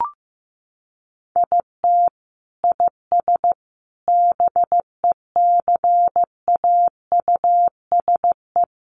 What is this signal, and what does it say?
Morse code, heard as a keyed tone, about 15 wpm, 700 Hz. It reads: IT IS BECAUSE
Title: Sentences from Top 600 Words 15wpm